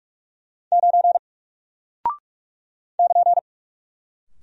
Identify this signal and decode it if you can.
Morse code, heard as a keyed tone, about 47 wpm, 700 Hz. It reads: 9 <KN>
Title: Single Character (Letter:Number:Punctuation:Prosign) Alt 45wpm